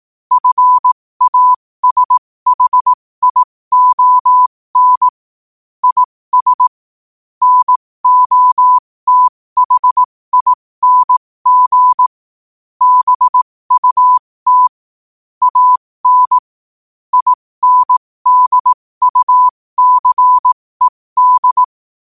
Morse code, heard as a keyed tone, about 18 wpm, spaced as for 15 wpm, 1000 Hz. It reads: FASHION IS NOTHING BUT AN INDUCED
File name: qotdfiles / 15wpm